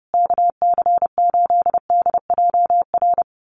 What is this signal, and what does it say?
Morse code, heard as a keyed tone, about 30 wpm, 700 Hz. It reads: KC8DJR
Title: USA Call Signs Repeated 2x 30wpm